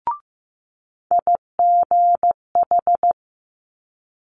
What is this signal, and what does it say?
Morse code, heard as a keyed tone, about 15 wpm, 700 Hz. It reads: IGH
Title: Multi-letter Phonemes Quick Fire 15wpm